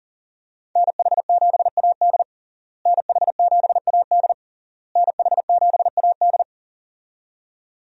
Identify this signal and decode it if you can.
Morse code, heard as a keyed tone, about 40 wpm, 700 Hz. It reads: NH7AD NH7AD NH7AD
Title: USA Call Signs - 5 Characters - Repeated 3x - 40wpm